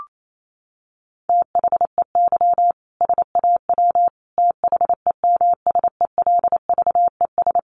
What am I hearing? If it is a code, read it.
Morse code, heard as a keyed tone, about 28 wpm, 700 Hz. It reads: THEY SAW THEMSELVES
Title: A Sound Of Thunder 28wpm